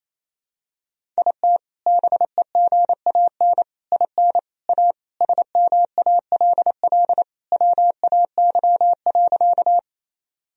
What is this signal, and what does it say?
Morse code, heard as a keyed tone, about 28 wpm, 700 Hz. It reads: IT BEGAN IN A SMALL WAY.
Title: Sentences from Top 1000 Words - Review Part 1 - 28wpm